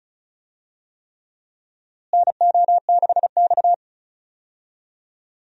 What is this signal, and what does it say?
Morse code, heard as a keyed tone, about 34 wpm, 700 Hz. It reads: NO6X
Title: USA Call Signs - 4 Characters - Repeated 3x - 35wpm